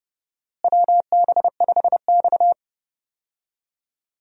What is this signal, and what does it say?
Morse code, heard as a keyed tone, about 29 wpm, 700 Hz. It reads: WB5X
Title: USA Call Signs - 4 Characters - Repeated 3x - 30wpm